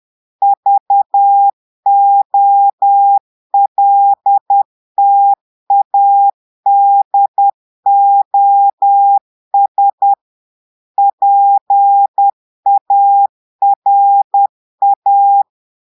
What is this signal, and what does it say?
Morse code, heard as a keyed tone, about 10 wpm, 800 Hz. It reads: VOLTADOS PARA
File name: portugais texte en clair 2 vitesse 10 mots minutes